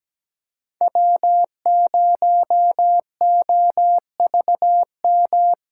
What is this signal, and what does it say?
Morse code, heard as a keyed tone, about 21 wpm, 700 Hz. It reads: W0OVM
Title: Speed Racing - USA 1x3 Call Signs 17wpm